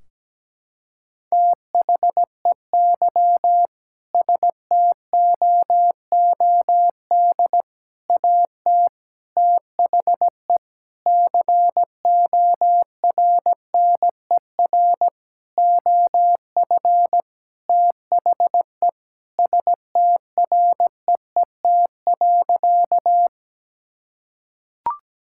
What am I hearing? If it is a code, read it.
Morse code, heard as a keyed tone, about 17 wpm, 700 Hz. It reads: THEY STOOD AT THE CORNER OF THE STREET.
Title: Sentences from Top 1600 Words 17wpm